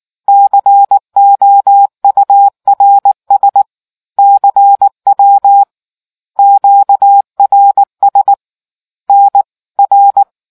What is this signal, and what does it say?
Morse code, heard as a keyed tone, about 19 wpm, 800 Hz. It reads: COURS CW QRS NR